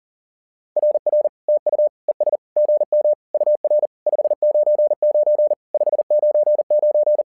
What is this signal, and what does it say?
Morse code, heard as a keyed tone, about 37 wpm, 600 Hz. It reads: RR TU ES GM UR 599 599
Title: SOTA QSO Practice 40wpm